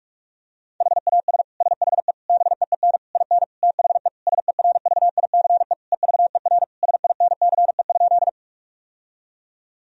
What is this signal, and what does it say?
Morse code, heard as a keyed tone, about 46 wpm, 700 Hz. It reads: HAS SHE BEEN IN THE SERVICE EVER SINCE?
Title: Sentences from Top 1000 Words - Review Part 4 - 45wpm